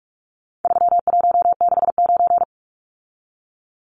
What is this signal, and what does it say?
Morse code, heard as a keyed tone, about 44 wpm, 700 Hz. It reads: 3169
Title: Numbers - 4-Digit 45wpm